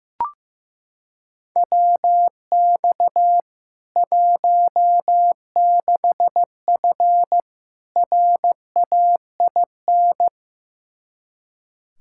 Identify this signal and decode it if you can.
Morse code, heard as a keyed tone, about 15 wpm, 700 Hz. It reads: WX 16F RAIN
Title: K1USN SST QSO Practice 15wpm